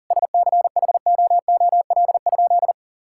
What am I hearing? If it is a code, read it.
Morse code, heard as a keyed tone, about 40 wpm, 700 Hz. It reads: SCHOOL?
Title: Sentences from Top 1000 Words - Review Part 5 - 40wpm